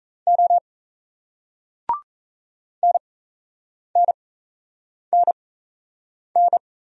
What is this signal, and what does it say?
Morse code, heard as a keyed tone, about 36 wpm, spaced as for 8 wpm, 700 Hz. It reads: O NNNN